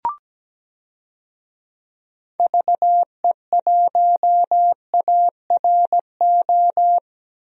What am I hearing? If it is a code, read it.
Morse code, heard as a keyed tone, about 17 wpm, 700 Hz. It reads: VE1ARO
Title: DXCC-100 Call Signs - 6 Characters - 17wpm